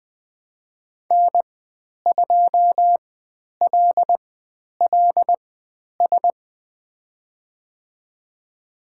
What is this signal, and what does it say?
Morse code, heard as a keyed tone, about 20 wpm, spaced as for 10 wpm, 700 Hz. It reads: N2LLS
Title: 095 Lesson 16g - Letter c - Call Signs 20wpm